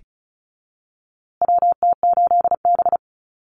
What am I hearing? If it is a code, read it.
Morse code, heard as a keyed tone, about 34 wpm, 700 Hz. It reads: WT8B
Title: USA Call Signs Repeated 3x 35wpm